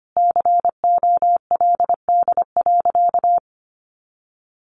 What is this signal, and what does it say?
Morse code, heard as a keyed tone, about 25 wpm, 700 Hz. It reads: COLD.
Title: Sentences from Top 1300 Words 25wpm